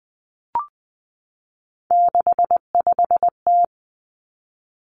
Morse code, heard as a keyed tone, about 19 wpm, 700 Hz. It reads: 65T
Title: Three Challenging Letter-Number Combinations 20wpm